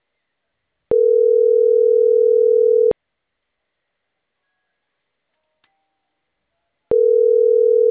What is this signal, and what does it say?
Ringback tone (North America), 440 + 480 Hz.